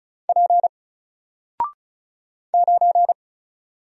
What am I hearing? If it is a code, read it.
Morse code, heard as a keyed tone, about 36 wpm, 700 Hz. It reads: P 9
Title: Single Character (Letter:Number:Punctuation:Prosign) Alt 35wpm